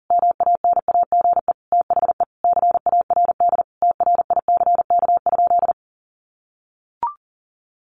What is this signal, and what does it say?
Morse code, heard as a keyed tone, about 40 wpm, 700 Hz. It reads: MANAGE THE CARD TRICK?
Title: Sentences from Top 1500 Words 40wpm